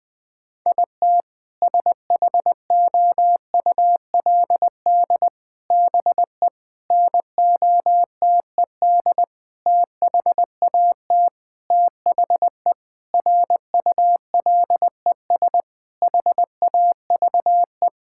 Morse code, heard as a keyed tone, about 20 wpm, 700 Hz. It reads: IT SHOULD BE NOTED THAT THE RULES HAVE
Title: Sentences with 5-Word Phrases - 20wpm